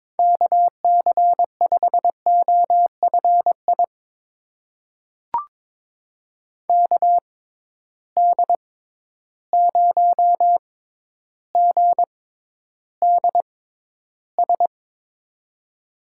Morse code, heard as a keyed tone, about 22 wpm, spaced as for 11 wpm, 700 Hz. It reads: KC5OFI KD0GDS
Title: USA Call Signs - 6 Characters - 6x Character Spacing - 22wpm